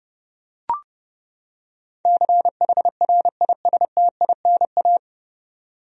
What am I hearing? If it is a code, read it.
Morse code, heard as a keyed tone, about 30 wpm, 700 Hz. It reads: CHRISTINA
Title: USA Common Names - 30wpm